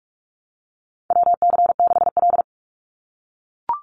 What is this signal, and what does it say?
Morse code, heard as a keyed tone, about 44 wpm, 700 Hz. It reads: WC6L
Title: USA Call Signs 45wpm